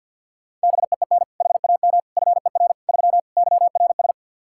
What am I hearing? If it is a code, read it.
Morse code, heard as a keyed tone, about 52 wpm, 700 Hz. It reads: BEEN HAM FER 3 YRS
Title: K1USN SST QSO Practice 50wpm